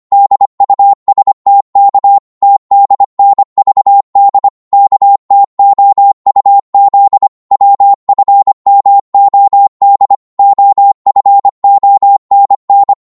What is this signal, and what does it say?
Morse code, heard as a keyed tone, about 25 wpm, 800 Hz. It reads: DUSTK TDNVD KTOUZ WFMOD OFONN
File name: Morse / ltr-25wpm